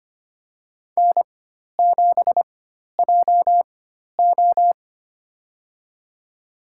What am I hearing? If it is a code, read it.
Morse code, heard as a keyed tone, about 25 wpm, spaced as for 12 wpm, 690 Hz. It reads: N7JO
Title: USA Call Signs - 4 Characters - 4x Character Spacing - 25wpm